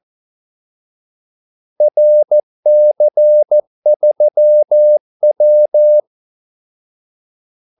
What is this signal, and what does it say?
Morse code, heard as a keyed tone, about 14 wpm, 600 Hz. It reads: RC3W
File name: Callsigns-14wpm